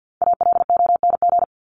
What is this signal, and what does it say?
Morse code, heard as a keyed tone, about 50 wpm, 700 Hz. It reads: ALONG
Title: Top 501-750 Words 50wpm